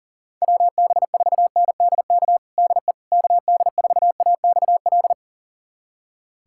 Morse code, heard as a keyed tone, about 40 wpm, 700 Hz. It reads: WB4NDK DE KD4AXL
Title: USA Callsign DE Callsign - 6 Characters - 40wpm